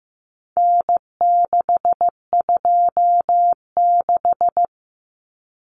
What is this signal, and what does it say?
Morse code, heard as a keyed tone, about 15 wpm, 700 Hz. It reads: N626
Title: Common Cut Numbers - 4-Digit 15wpm